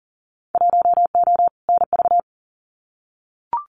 Morse code, heard as a keyed tone, about 39 wpm, 700 Hz. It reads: 1O NV
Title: Winter Field Day Exchanges Encore 40wpm